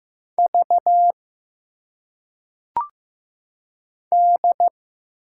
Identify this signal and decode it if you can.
Morse code, heard as a keyed tone, about 15 wpm, 700 Hz. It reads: V D
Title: Single Letters - Phonetic Pronunciation 15wpm